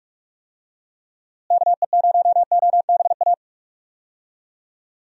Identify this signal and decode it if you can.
Morse code, heard as a keyed tone, about 46 wpm, 700 Hz. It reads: KE0OBA
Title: USA Call Signs - 6 Characters - Repeated 2x - 45wpm